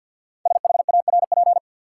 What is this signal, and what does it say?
Morse code, heard as a keyed tone, about 53 wpm, 700 Hz. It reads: SHARP
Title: Sentences from Top 1600 Words - New Word Drill - 50wpm